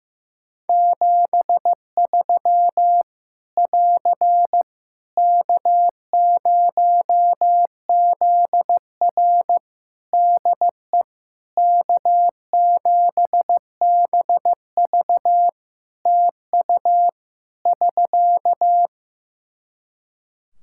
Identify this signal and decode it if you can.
Morse code, heard as a keyed tone, about 15 wpm, 700 Hz. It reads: 73 <AR> K0ZR DE K7BV TU <SK>
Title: K1USN SST QSO Practice 15wpm